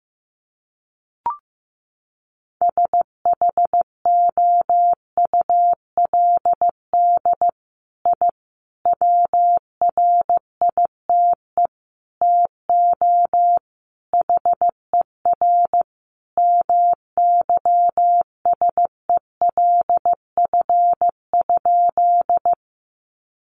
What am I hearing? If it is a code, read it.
Morse code, heard as a keyed tone, about 15 wpm, 700 Hz. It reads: SHOULD I WRITE TO HER MYSELF?
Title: Sentences from Top 1300 Words 15wpm